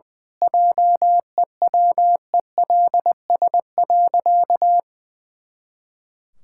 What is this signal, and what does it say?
Morse code, heard as a keyed tone, about 20 wpm, 700 Hz. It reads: JEWELS.
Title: Pangram Challenge v2 - 20wpm